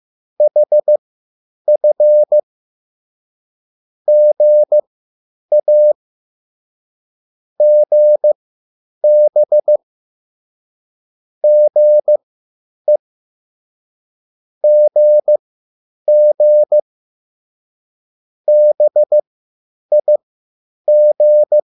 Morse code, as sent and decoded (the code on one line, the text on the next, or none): .... ..-. / --. .- / --. -... / --. / --. --. / -... .. --.
HF GA GB GE GG BIG